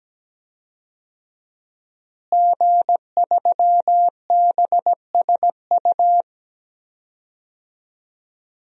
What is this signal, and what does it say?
Morse code, heard as a keyed tone, about 17 wpm, 700 Hz. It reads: G3BSU
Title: DXCC-100 Call Signs - 5 Characters - Repeated 3x - 17wpm